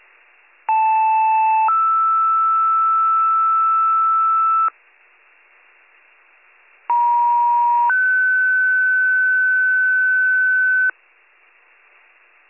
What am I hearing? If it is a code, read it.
Two-tone page, heard as a one-second tone, then a three-second tone, about 880 then 1320 Hz, about 950 then 1500 Hz. It is heard from 0.7 to 10.9 s.